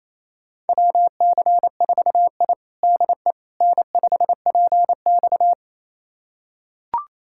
Morse code, heard as a keyed tone, about 28 wpm, 700 Hz. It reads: WC4I DE N5PX
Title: USA Callsign DE Callsign - 4 Characters - 28wpm